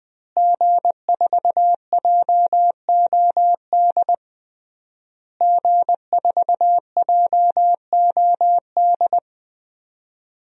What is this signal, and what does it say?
Morse code, heard as a keyed tone, about 20 wpm, 700 Hz. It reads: G4JOD G4JOD
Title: DXCC-100 Call Signs - 5 Characters - Repeated 3x - 20wpm